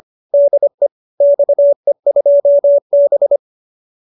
Morse code, heard as a keyed tone, about 25 wpm, 570 Hz. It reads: DE XE2B